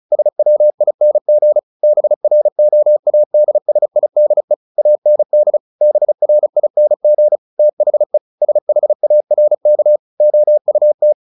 Morse code, heard as a keyed tone, about 34 wpm, 600 Hz. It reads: SWING BROADSIDE AND BRING THE SHARK OUT
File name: the old man and the sea 2 35wpm